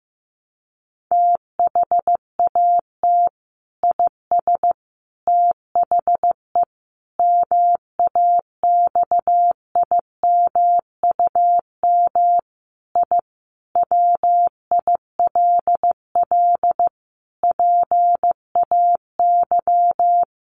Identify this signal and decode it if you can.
Morse code, heard as a keyed tone, about 15 wpm, 700 Hz. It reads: THAT IS THE MAXIMUM I WILL PAY
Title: Sentences from Top 1300 Words 15wpm